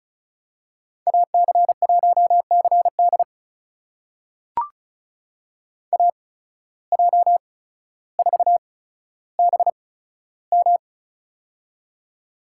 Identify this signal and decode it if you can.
Morse code, heard as a keyed tone, about 36 wpm, spaced as for 15 wpm, 700 Hz. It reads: AC1CD AJ4BM
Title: USA Call Signs - 8x Character Spacing - 35wpm